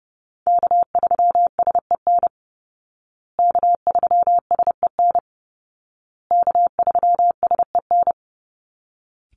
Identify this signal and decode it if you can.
Morse code, heard as a keyed tone, about 30 wpm, 700 Hz. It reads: K3SEN K3SEN K3SEN
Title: Call Signs Repeated 2x - 30wpm